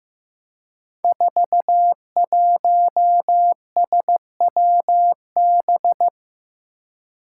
Morse code, heard as a keyed tone, about 15 wpm, 700 Hz. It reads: 41SWB
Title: Five Challenging Letter-Number Combinations 15wpm